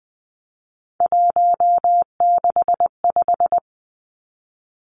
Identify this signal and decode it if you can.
Morse code, heard as a keyed tone, about 20 wpm, 700 Hz. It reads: 165
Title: Three Challenging Letter-Number Combinations 20wpm